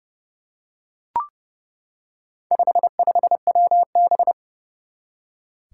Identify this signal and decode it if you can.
Morse code, heard as a keyed tone, about 31 wpm, 700 Hz. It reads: 55WB
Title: Four Challenging Letter-Number Combinations 30wpm